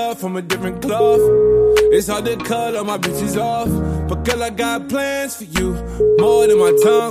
Ringback tone (Europe, 425 Hz), with noise.